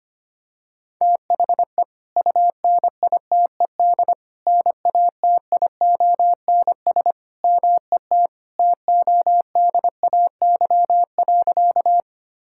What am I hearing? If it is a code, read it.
Morse code, heard as a keyed tone, about 25 wpm, 700 Hz. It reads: THE UNITED NATIONS MET TODAY.
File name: practice / 25wpm / Sentences with 3-Word Phrases - 25wpm